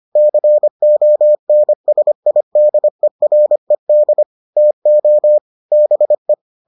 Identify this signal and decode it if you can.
Morse code, heard as a keyed tone, about 25 wpm, 600 Hz. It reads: CONSIDERED TO BE